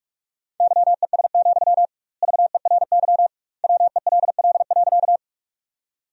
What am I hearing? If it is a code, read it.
Morse code, heard as a keyed tone, about 43 wpm, 700 Hz. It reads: YES, VERY WELL.
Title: 2x Word-Spacing - Sentences from Top 500 Words - 45wpm